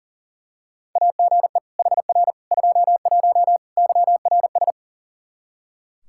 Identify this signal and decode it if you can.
Morse code, heard as a keyed tone, about 40 wpm, 700 Hz. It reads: AGE HR 21 YRS
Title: Practice QSO - Exchange 3 - More Info - 40wpm